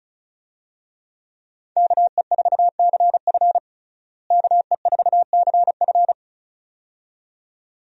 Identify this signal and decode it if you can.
Morse code, heard as a keyed tone, about 36 wpm, 700 Hz. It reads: KE4CF KE4CF
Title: USA Call Signs - 5 Characters - Repeated 2x - 35wpm